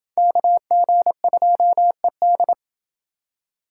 Morse code, heard as a keyed tone, about 29 wpm, 700 Hz. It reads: KG2ED